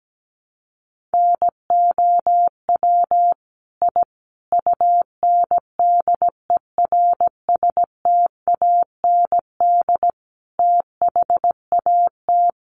Morse code, heard as a keyed tone, about 17 wpm, 700 Hz. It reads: NOW I UNDERSTAND THAT
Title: Sentences from Top 1000 Words 17wpm